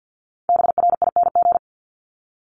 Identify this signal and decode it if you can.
Morse code, heard as a keyed tone, about 50 wpm, 700 Hz. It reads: BRING
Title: Multi-letter Phonemes 50wpm